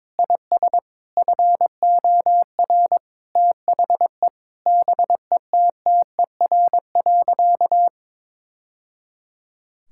Morse code, heard as a keyed tone, about 22 wpm, 700 Hz. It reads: IS FOR THE BETTER.
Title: Sentences from Top 1000 Words - Review Part 5 - 22wpm